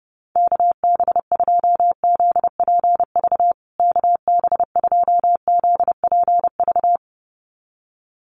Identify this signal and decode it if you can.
Morse code, heard as a keyed tone, about 29 wpm, 700 Hz. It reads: KB2ZPV KB2ZPV
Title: USA Call Signs Repeated 2x 30wpm